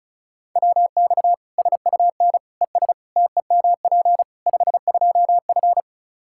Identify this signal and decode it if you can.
Morse code, heard as a keyed tone, about 34 wpm, 700 Hz. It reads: WX SUN ES TEMP 52F
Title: Practice QSO - Exchange 2 - Info - 35wpm